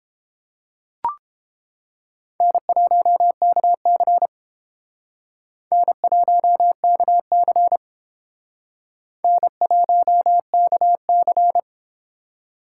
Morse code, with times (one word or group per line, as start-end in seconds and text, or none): 2.4-4.3 N1KC
5.7-7.8 N1KC
9.2-11.6 N1KC